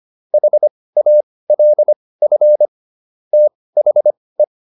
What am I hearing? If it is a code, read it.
Morse code, heard as a keyed tone, about 25 wpm, spaced as for 18 wpm, 600 Hz. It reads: HALF THE